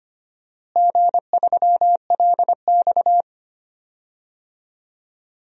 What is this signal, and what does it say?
Morse code, heard as a keyed tone, about 25 wpm, 700 Hz. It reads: G3LX
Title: DXCC-100 Call Signs - 4 Characters - Repeated 3x - 25wpm